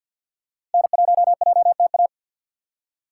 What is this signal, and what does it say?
Morse code, heard as a keyed tone, about 49 wpm, 700 Hz. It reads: N1JTA
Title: USA Call Signs - 5 Characters - 4x Character Spacing - 50wpm